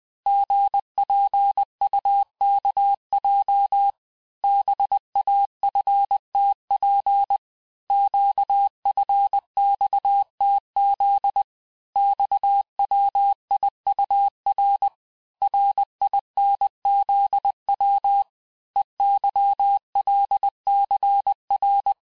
Morse code, heard as a keyed tone, about 20 wpm, 800 Hz. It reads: GPUKJ BAFTP QFXTZ XWIUR RINZW EYLCR
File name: cours-du-lundi files / cours-1584-teste-chiffre-vitesse-19mpm